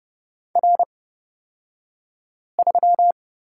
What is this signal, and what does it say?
Morse code, heard as a keyed tone, about 30 wpm, spaced as for 5 wpm, 700 Hz. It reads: R3
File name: Lesson 37d - Letter x - Two Characters 30wpm